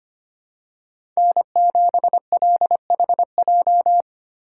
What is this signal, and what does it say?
Morse code, heard as a keyed tone, about 25 wpm, 700 Hz. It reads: N7LHJ
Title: USA Call Signs - 5 Characters - 2x Character Spacing - 25wpm